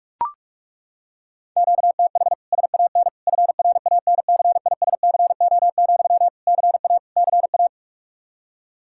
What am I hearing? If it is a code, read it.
Morse code, heard as a keyed tone, about 41 wpm, 700 Hz. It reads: QTH SAN FRANCISCO, CA CA
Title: K1USN SST QSO Practice 45wpm